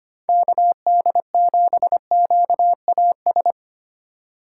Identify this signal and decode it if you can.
Morse code, heard as a keyed tone, about 25 wpm, 700 Hz. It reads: KD7QAS
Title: USA Call Signs - 6 Characters - 3x Character Spacing - 25wpm